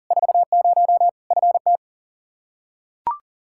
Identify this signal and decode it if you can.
Morse code, heard as a keyed tone, about 40 wpm, 700 Hz. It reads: 40 FT
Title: Practice QSO - Exchange 2 - Info - 40wpm